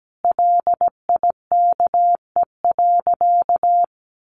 Morse code, heard as a keyed tone, about 17 wpm, 700 Hz. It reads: LIKE.
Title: Sentences from Top 1300 Words 17wpm